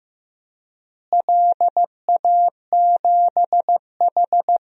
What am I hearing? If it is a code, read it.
Morse code, heard as a keyed tone, about 15 wpm, 700 Hz. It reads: LA7H
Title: DXCC-100 Call Signs - 4 Characters - Repeated 3x - 15wpm